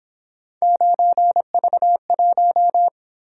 Morse code, heard as a keyed tone, about 26 wpm, 700 Hz. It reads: 9V1
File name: Speed-Racing - Three Letter-Number Combinations - 22wpm